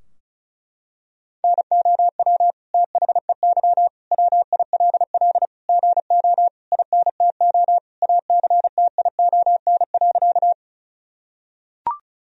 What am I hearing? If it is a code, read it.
Morse code, heard as a keyed tone, about 36 wpm, 700 Hz. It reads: NOW THEY WILL GO INTO ACTION.
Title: Sentences from Top 1000 Words - Review Part 4 - 35wpm